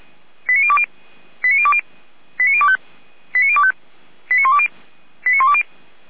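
Five-tone selective call (ZVEI), heard as quick runs of five tones, tones of about 70 ms. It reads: 89020 89020 89025 89025 89120 89120
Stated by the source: Five-tone ZVEI selective call received by SDR.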